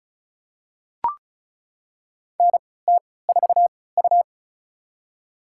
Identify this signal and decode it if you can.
Morse code, heard as a keyed tone, about 34 wpm, spaced as for 20 wpm, 700 Hz. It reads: NT4U
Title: USA Call Signs - 4 Characters - 3x Character Spacing - 35wpm